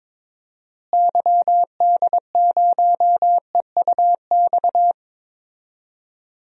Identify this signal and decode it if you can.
Morse code, heard as a keyed tone, about 22 wpm, 700 Hz. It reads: YD0EUX
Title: DXCC-100 Call Signs - 6 Characters - Repeated 3x - 22wpm